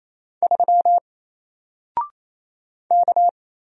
Morse code, heard as a keyed tone, about 27 wpm, 700 Hz. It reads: 3 K
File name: Single-Letter-Number-28wpm